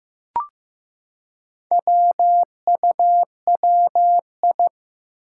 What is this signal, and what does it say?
Morse code, heard as a keyed tone, about 15 wpm, 700 Hz. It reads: WUWI
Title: Four Challenging Letter-Number Combinations 15wpm